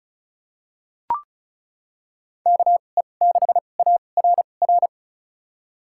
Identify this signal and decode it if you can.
Morse code, heard as a keyed tone, about 36 wpm, spaced as for 25 wpm, 700 Hz. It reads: KE6ARR
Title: USA Call Signs - 6 Characters - 2x Character Spacing - 35wpm